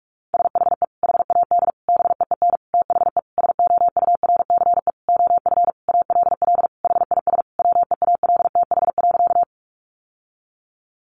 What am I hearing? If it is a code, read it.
Morse code, heard as a keyed tone, about 44 wpm, 700 Hz. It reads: SHE HAD BEEN THE SOURCE OF ALL HIS WEALTH.
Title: Sentences from Top 1200 Words 45wpm